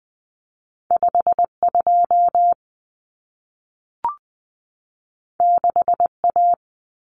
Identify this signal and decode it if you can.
Morse code, heard as a keyed tone, about 20 wpm, 700 Hz. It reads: 52 6A
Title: Common Cut Numbers - 2-Digit 20wpm